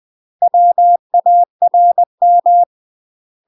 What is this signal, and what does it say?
Morse code, heard as a keyed tone, about 20 wpm, 700 Hz. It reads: WARM
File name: UK MASTER QSOs TRAINING / 20WPM / UK MASTER QSOs 0073